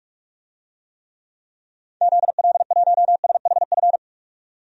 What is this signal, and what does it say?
Morse code, heard as a keyed tone, about 42 wpm, 700 Hz. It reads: ZL1SHF
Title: DXCC-100 Call Signs - 6 Characters - 45wpm